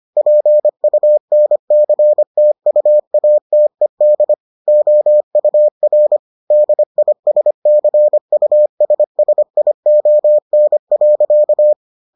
Morse code, heard as a keyed tone, about 25 wpm, 600 Hz. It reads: PUNCTUATED OUR DISCUSSION.